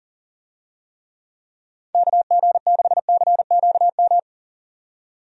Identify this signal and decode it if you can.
Morse code, heard as a keyed tone, about 40 wpm, 700 Hz. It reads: KG6CQM
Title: USA Call Signs - 6 Characters - 40wpm